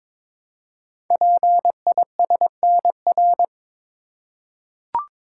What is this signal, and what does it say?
Morse code, heard as a keyed tone, about 23 wpm, 700 Hz. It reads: PISNR
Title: Code Groups - 5 Letters - 22wpm